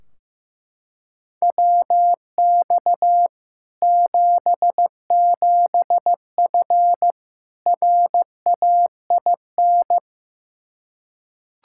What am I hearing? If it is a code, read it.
Morse code, heard as a keyed tone, about 16 wpm, 700 Hz. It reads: WX 77F RAIN
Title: Practice QSO - Exchange 2 - Info - 15wpm